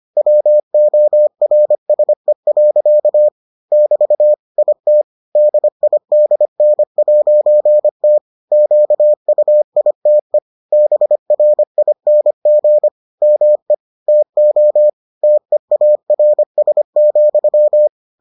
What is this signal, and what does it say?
Morse code, heard as a keyed tone, about 25 wpm, 600 Hz. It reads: WORSE. <BT> IT DIDN'T QUITE BRING ME TO TEARS,